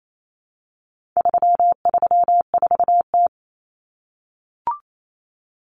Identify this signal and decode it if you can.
Morse code, heard as a keyed tone, about 28 wpm, 700 Hz. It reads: 334T
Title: Common Cut Numbers - 4-Digit 28wpm